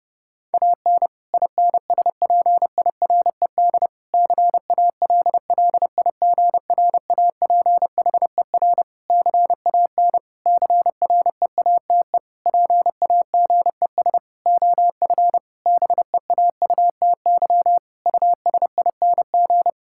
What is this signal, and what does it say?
Morse code, heard as a keyed tone, about 30 wpm, 700 Hz. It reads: AN INSPIRED CALLIGRAPHER CAN CREATE PAGES OF BEAUTY USING
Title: Pangram Challenge v2 - 30wpm